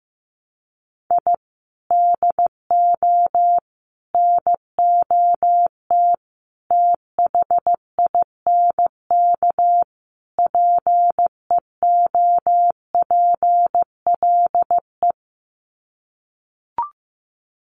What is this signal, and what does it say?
Morse code, heard as a keyed tone, about 15 wpm, 700 Hz. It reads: I DO NOT THINK PEOPLE
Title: Sets of 5 Words Encore2 15wpm